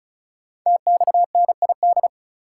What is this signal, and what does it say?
Morse code, heard as a keyed tone, about 35 wpm, 700 Hz. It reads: TXNID
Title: Code Groups - 5 Letter:Numbers - 35wpm